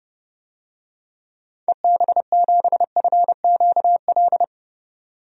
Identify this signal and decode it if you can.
Morse code, heard as a keyed tone, about 32 wpm, 700 Hz. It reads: EB7FQL
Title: DXCC-100 Call Signs - 6 Characters - 30wpm